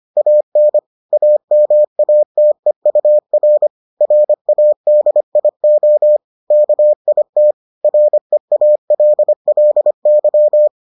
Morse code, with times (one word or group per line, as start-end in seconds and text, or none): 0.2-0.8 AN
1.1-3.7 AMATEUR
4.0-6.2 RADIO
6.5-7.5 KIT
7.8-10.7 REALLY